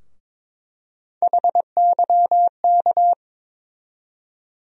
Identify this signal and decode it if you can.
Morse code, heard as a keyed tone, about 22 wpm, 700 Hz. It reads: HYK
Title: USA Call Sign Postfixes - 3 Letters 22wpm